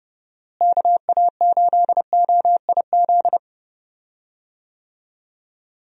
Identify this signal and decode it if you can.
Morse code, heard as a keyed tone, about 29 wpm, 700 Hz. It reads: KA8OIZ
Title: USA Call Signs - 6 Characters - Repeated 3x - 30wpm